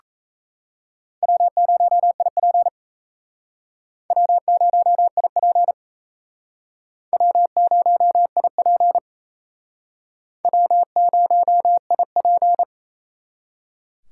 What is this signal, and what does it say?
Morse code, heard as a keyed tone, about 34 wpm, 700 Hz. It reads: W0IP W0IP W0IP W0IP
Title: Speed Racing - USA 1x2 Call Signs 28wpm